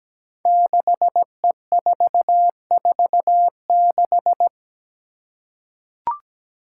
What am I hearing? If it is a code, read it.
Morse code, heard as a keyed tone, about 17 wpm, 700 Hz. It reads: 6E446
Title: Five Challenging Letter-Number Combinations 17wpm